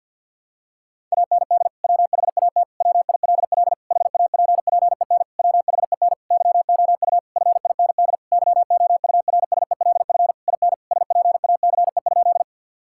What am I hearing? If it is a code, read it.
Morse code, heard as a keyed tone, about 50 wpm, 700 Hz. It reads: AND WHAT WILL HAPPEN WHEN YOU FIND YOURSELF IN SPACE?
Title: Sentences from Top 1000 Words - Review Part 5 - 50wpm